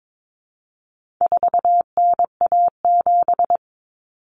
Morse code, heard as a keyed tone, about 22 wpm, 700 Hz. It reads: 4NA7
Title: Common Cut Numbers - 4-Digit 22wpm